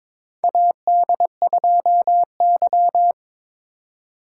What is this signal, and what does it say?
Morse code, heard as a keyed tone, about 22 wpm, 700 Hz. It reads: AD2Y
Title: USA Call Signs - 4 Characters - 4x Character Spacing - 22wpm